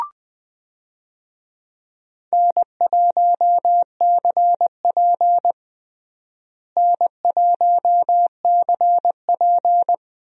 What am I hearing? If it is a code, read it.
Morse code, heard as a keyed tone, about 20 wpm, 700 Hz. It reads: N1CP N1CP
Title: USA Call Signs - 4 Characters - Repeated 2x - 20wpm